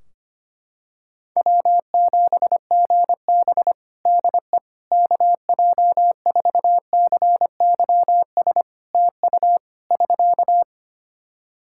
Morse code, heard as a keyed tone, about 25 wpm, 700 Hz. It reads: W7GB DE KJ4CYS TU <SK>
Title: Practice QSO - Exchange 4 - Ending - 25wpm